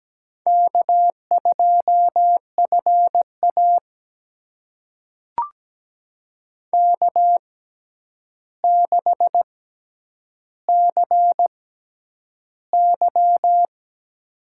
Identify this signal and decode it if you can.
Morse code, heard as a keyed tone, about 17 wpm, spaced as for 9 wpm, 700 Hz. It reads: K2FA K6CY
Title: USA Call Signs - 4 Characters - 6x Character Spacing - 17wpm